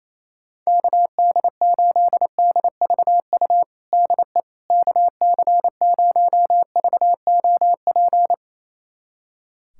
Morse code, heard as a keyed tone, about 28 wpm, 700 Hz. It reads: KD8DVU DE KC0VOP
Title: USA Callsign DE Callsign 28wpm